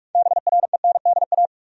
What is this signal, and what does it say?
Morse code, heard as a keyed tone, about 46 wpm, 700 Hz. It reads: BRENDA
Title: USA Common Names - 45wpm